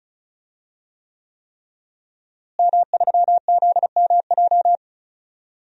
Morse code, heard as a keyed tone, about 36 wpm, 700 Hz. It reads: M3ZMJ
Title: DXCC-100 Call Signs - 5 Characters - Repeated 3x - 35wpm